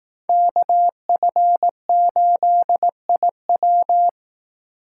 Morse code, heard as a keyed tone, about 18 wpm, 700 Hz. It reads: KF8IW
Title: Speed Racing - USA 2x2 Call Signs 15wpm